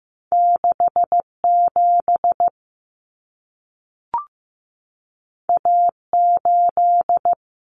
Morse code, heard as a keyed tone, about 15 wpm, 700 Hz. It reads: 67 A8
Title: Common Cut Numbers - 2-Digit 15wpm